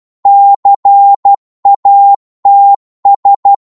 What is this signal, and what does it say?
Morse code, heard as a keyed tone, about 12 wpm, 800 Hz. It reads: CATS
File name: Morse / psg-12wpm